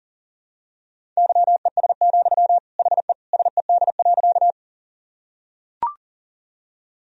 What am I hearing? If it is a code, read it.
Morse code, heard as a keyed tone, about 40 wpm, 700 Hz. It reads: YES, HE SED.
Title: Sentences from Top 500 Words and Abbreviations 40wpm